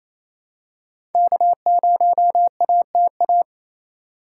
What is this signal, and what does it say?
Morse code, heard as a keyed tone, about 28 wpm, 700 Hz. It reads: K0ATA
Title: USA Call Signs - 5 Characters - 2x Character Spacing - 28wpm